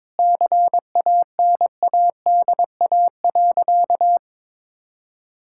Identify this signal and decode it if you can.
Morse code, heard as a keyed tone, about 22 wpm, 700 Hz. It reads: CANADA.
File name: practice / 22wpm / Sentences with 4-Word Phrases - 22wpm